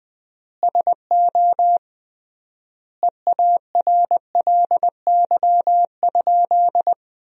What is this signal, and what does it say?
Morse code, heard as a keyed tone, about 20 wpm, 700 Hz. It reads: SO EARLY?
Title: 3x Word-Spacing - Sentences from Top 400 Words - 20wpm